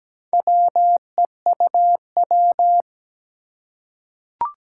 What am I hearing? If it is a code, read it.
Morse code, heard as a keyed tone, about 17 wpm, 700 Hz. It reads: WEUW
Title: Four Challenging Letter-Number Combinations 17wpm